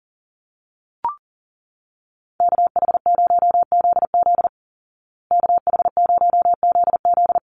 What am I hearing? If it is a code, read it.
Morse code, heard as a keyed tone, about 40 wpm, 700 Hz. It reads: KH0ZZ KH0ZZ
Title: Call Signs Repeated 2x - 40wpm